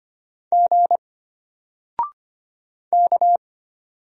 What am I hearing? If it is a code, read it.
Morse code, heard as a keyed tone, about 25 wpm, 700 Hz. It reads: G K
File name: Single-Letter-Number-25wpm